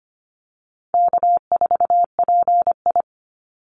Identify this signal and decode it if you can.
Morse code, heard as a keyed tone, about 25 wpm, 700 Hz. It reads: K4PI
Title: Call Signs Repeated 3x - 25wpm